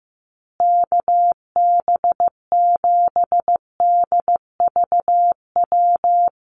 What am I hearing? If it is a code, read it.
Morse code, heard as a keyed tone, about 15 wpm, 700 Hz. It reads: KB7DVW
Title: USA Call Signs Repeated 3x 15wpm